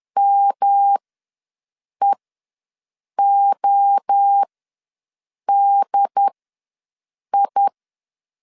Morse code, heard as a keyed tone, about 10 wpm, spaced as for 6 wpm, 800 Hz. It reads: MEODI
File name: LEZIONI / 7 GF9Z / 7 GF9 ws 10wpm 4034